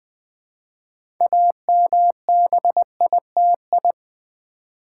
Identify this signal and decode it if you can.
Morse code, heard as a keyed tone, about 20 wpm, 700 Hz. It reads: AMBITI
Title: Build A Word - 3 Letters At A Time - 9-Letter Words - 20wpm